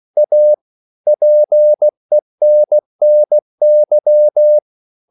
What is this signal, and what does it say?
Morse code, heard as a keyed tone, about 16 wpm, 600 Hz. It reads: A PENNY
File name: CW / Practice / EnglishSayings16